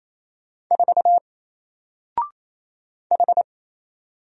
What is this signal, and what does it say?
Morse code, heard as a keyed tone, about 27 wpm, 700 Hz. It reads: 4 H
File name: Single-Letter-Number-28wpm